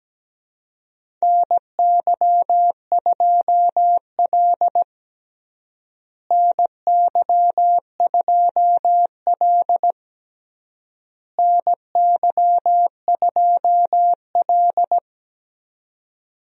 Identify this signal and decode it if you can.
Morse code, heard as a keyed tone, about 17 wpm, 700 Hz. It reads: NY2L NY2L NY2L
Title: USA Call Signs - 4 Characters - Repeated 3x - 17wpm